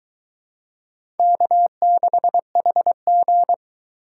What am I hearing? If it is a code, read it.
Morse code, heard as a keyed tone, about 23 wpm, 700 Hz. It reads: K6HG